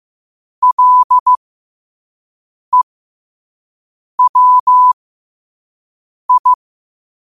Morse code, heard as a keyed tone, about 15 wpm, spaced as for 5 wpm, 1000 Hz. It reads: LEWI
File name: qotdfiles / 5wpm